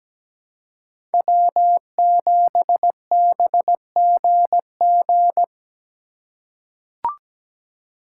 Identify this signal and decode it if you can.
Morse code, heard as a keyed tone, about 17 wpm, 700 Hz. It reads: W7BGG
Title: USA Call Signs - 3x Character Spacing - 17wpm